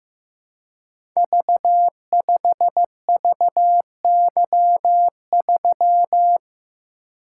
Morse code, heard as a keyed tone, about 15 wpm, 700 Hz. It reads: V5VY3
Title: Code Groups - 5 Letter:Numbers - 15wpm